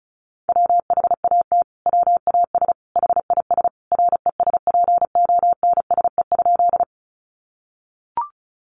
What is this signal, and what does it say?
Morse code, heard as a keyed tone, about 34 wpm, 700 Hz. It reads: WHAT WAS HIS RESPONSE?
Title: Sentences from Top 1400 Words 35wpm